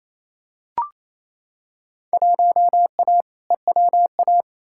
Morse code, heard as a keyed tone, about 28 wpm, 700 Hz. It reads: 1A EWA
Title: Field Day Exchanges Encore - 28wpm